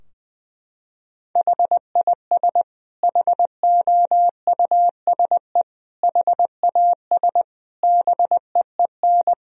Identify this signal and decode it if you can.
Morse code, heard as a keyed tone, about 20 wpm, 700 Hz. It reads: HIS HOUSE HAS BEEN
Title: Sentences from Top 1000 Words - Review Part 3 - 20wpm